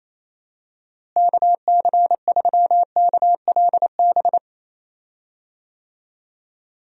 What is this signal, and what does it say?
Morse code, heard as a keyed tone, about 28 wpm, 700 Hz. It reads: KC3KLB
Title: USA Call Signs - 6 Characters - Repeated 3x - 28wpm